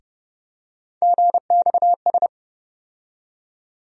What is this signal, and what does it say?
Morse code, heard as a keyed tone, about 30 wpm, 700 Hz. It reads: GXS
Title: USA Call Sign Postfixes - 3 Letters 30wpm